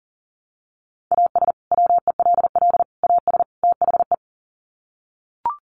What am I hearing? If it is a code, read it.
Morse code, heard as a keyed tone, about 40 wpm, 700 Hz. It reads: AS WELL AS THE
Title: Sets of 4 Words 40wpm